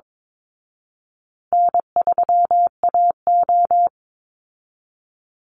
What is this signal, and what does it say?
Morse code, heard as a keyed tone, about 22 wpm, 700 Hz. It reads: N3AO
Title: Easy Call Signs Repeated 3x - 22wpm